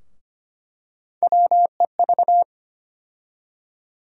Morse code, heard as a keyed tone, about 25 wpm, 700 Hz. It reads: WEV
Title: USA Call Sign Postfixes - 3 Letters 25wpm